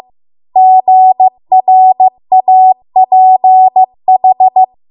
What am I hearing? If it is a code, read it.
Morse code, heard as a keyed tone, about 15 wpm, 750 Hz. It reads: GRAPH